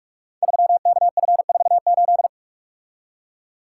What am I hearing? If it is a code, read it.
Morse code, heard as a keyed tone, about 44 wpm, 700 Hz. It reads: 3KF48
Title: Code Groups - 5 Letter:Numbers - 45wpm